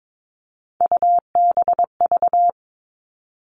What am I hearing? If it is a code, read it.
Morse code, heard as a keyed tone, about 22 wpm, 700 Hz. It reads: UBV
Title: All Cut Numbers - 3-Digit 22wpm